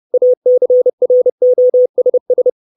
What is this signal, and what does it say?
Morse code, heard as a keyed tone, about 30 wpm, 500 Hz. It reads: ACROSS
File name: 632 Words 30WPM